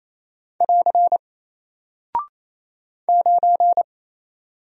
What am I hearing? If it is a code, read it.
Morse code, heard as a keyed tone, about 28 wpm, 700 Hz. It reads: <AR> 9
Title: Single Character (Letter:Number:Punctuation:Prosign) 28wpm